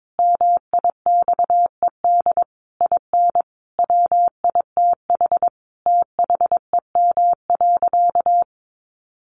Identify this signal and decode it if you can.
Morse code, heard as a keyed tone, about 22 wpm, 700 Hz. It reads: MIXED IN WITH THEM.
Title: Sentences from Top 1500 Words 22wpm